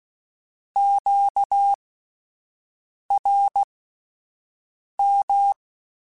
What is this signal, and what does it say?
Morse code, heard as a keyed tone, about 16 wpm, spaced as for 5 wpm, 800 Hz. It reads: QRM